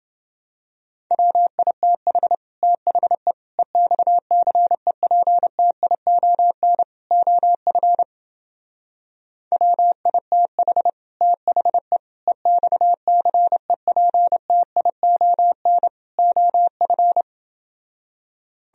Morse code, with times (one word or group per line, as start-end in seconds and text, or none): 1.1-2.4 WITH
2.6-3.3 THE
3.6-6.8 EXCEPTION
7.1-8.0 OF
9.5-10.9 WITH
11.2-12.0 THE
12.3-15.9 EXCEPTION
16.2-17.2 OF